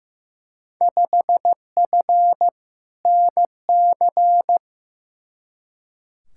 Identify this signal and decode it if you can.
Morse code, heard as a keyed tone, about 15 wpm, 700 Hz. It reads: 5F NC
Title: Field Day Exchanges Encore - 15wpm